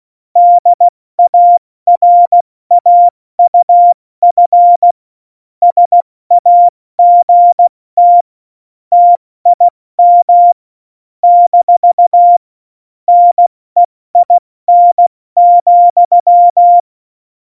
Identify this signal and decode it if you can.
Morse code, heard as a keyed tone, about 16 wpm, 700 Hz. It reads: DARAUF SAGT TIM - NEIN,
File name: Morsekurs / Witze / Witz 07. November 2017, 70 F80